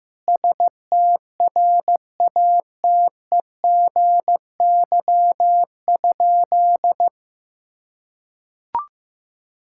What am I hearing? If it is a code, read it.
Morse code, heard as a keyed tone, about 15 wpm, 700 Hz. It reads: STRATEGY?
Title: Sentences from Top 1000 Words - Review Part 5 - 15wpm